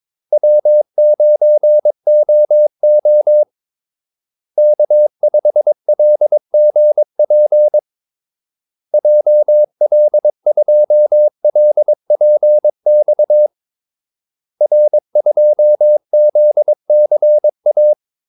Morse code, heard as a keyed tone, about 22 wpm, 600 Hz. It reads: W9OO K5LGP JL2LPX R2ZCA
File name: Callsigns-22wpm